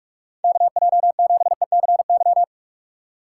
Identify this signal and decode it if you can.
Morse code, heard as a keyed tone, about 46 wpm, 700 Hz. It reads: KJ7ECY
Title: USA Call Signs - 2x Character Spacing - 45wpm